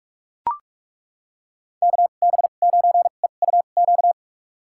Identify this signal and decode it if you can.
Morse code, heard as a keyed tone, about 41 wpm, spaced as for 31 wpm, 700 Hz. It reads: KB9EUQ
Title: USA Call Signs - 2x Character Spacing - 45wpm